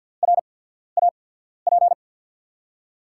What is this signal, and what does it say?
Morse code, heard as a keyed tone, about 52 wpm, spaced as for 14 wpm, 700 Hz. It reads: RAP
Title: USA Call Signs - 5 Characters - 8x Character Spacing - 50wpm